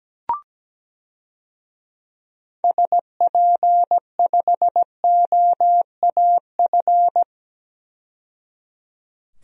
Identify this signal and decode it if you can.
Morse code, heard as a keyed tone, about 17 wpm, 700 Hz. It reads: SP5OAF
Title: DXCC-100 Call Signs - 6 Characters - 17wpm